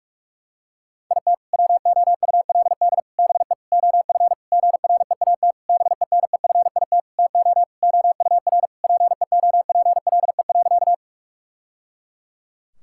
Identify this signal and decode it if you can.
Morse code, heard as a keyed tone, about 46 wpm, 700 Hz. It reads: IT WOULD BE OF GREAT BENEFIT TO OUR PEOPLE.
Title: Sentences from Top 1000 Words - Review Part 6 - 45wpm